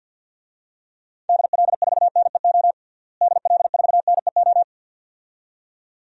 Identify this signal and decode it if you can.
Morse code, heard as a keyed tone, about 52 wpm, 700 Hz. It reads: DL4NEO DL4NEO
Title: DXCC-100 Call Signs - 6 Characters - Repeated 2x - 50wpm